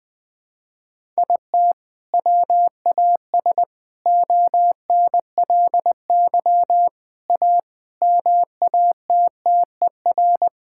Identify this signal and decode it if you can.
Morse code, heard as a keyed tone, about 20 wpm, 700 Hz. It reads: IT WAS ONLY A MATTER
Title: Sets of 8 Words - 20wpm